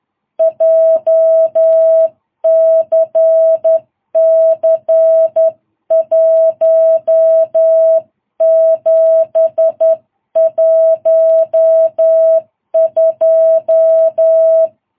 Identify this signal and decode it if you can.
Morse code, heard as a keyed tone, about 10 wpm, 660 Hz. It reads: JCC 1712
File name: jh8cbh / 2022cw